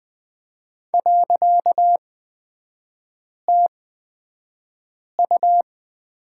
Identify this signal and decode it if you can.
Morse code, heard as a keyed tone, about 20 wpm, spaced as for 5 wpm, 700 Hz. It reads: .TU
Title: 255 Lesson 41e - Period - Three Characters 20wpm